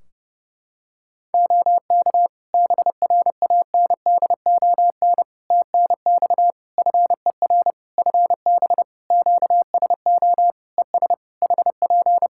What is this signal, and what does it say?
Morse code, heard as a keyed tone, about 28 wpm, 700 Hz. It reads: OK BRANDON TNX FER FB QSO ES HP
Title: K1USN SST QSO Practice 30wpm